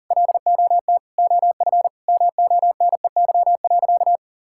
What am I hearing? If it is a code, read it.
Morse code, heard as a keyed tone, about 37 wpm, 700 Hz. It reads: LOT OF MONEY.
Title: Sentences from Top 1000 Words - Review Part 5 - 40wpm